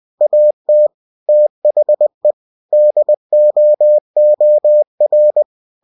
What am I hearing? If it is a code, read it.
Morse code, heard as a keyed tone, about 20 wpm, 600 Hz. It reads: AT THE DOOR
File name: cw-books / winnie-the-pooh / winnie-20wpm-page-0078